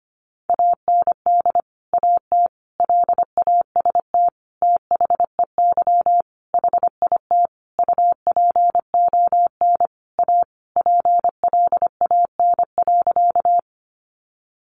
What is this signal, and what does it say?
Morse code, heard as a keyed tone, about 25 wpm, 700 Hz. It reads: AND AT LAST THEY HIT UPON A PLAN.
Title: Sentences from Top 1200 Words 25wpm